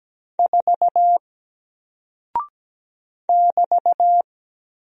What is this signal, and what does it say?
Morse code, heard as a keyed tone, about 17 wpm, 700 Hz. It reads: 4 <BT>
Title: Single Character (Letter:Number:Punctuation:Prosign) 17wpm